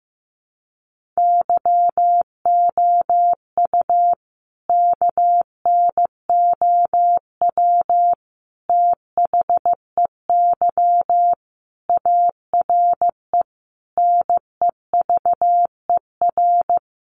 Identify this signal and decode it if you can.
Morse code, heard as a keyed tone, about 15 wpm, 700 Hz. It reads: YOU KNOW THEY ARE NEVER
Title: Sentences from Top 1400 Words 15wpm